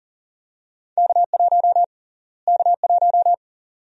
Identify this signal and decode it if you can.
Morse code, heard as a keyed tone, about 40 wpm, 700 Hz. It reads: K1 K1
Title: USA Call Sign Prefixes - Repeated 2x - 40wpm